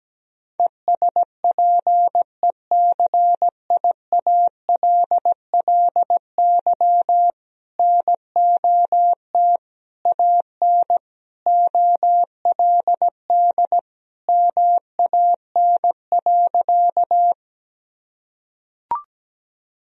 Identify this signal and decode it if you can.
Morse code, heard as a keyed tone, about 17 wpm, 700 Hz. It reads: ESPECIALLY NOT AN OLD MAN.
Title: Sentences from Top 1000 Words - Review Part 4 - 17wpm